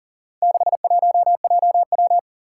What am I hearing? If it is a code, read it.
Morse code, heard as a keyed tone, about 40 wpm, 700 Hz. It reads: 61JW
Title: Four Challenging Letter-Number Combinations 40wpm